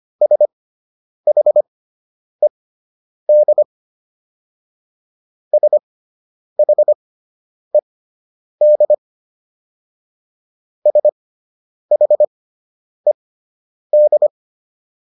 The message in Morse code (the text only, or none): SHED SHED SHED